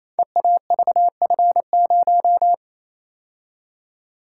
Morse code, heard as a keyed tone, about 29 wpm, 700 Hz. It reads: EAVF0
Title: Code Groups - 5 Letter:Numbers - 28wpm